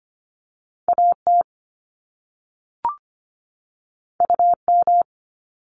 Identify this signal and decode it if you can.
Morse code, heard as a keyed tone, about 25 wpm, 700 Hz. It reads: AT UM
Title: Top Two Letter Combinations in Words 25wpm